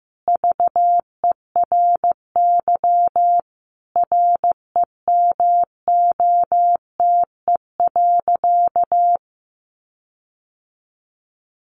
Morse code, heard as a keyed tone, about 15 wpm, 700 Hz. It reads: VERY REMOTE.
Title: Sentences from Top 1300 Words 15wpm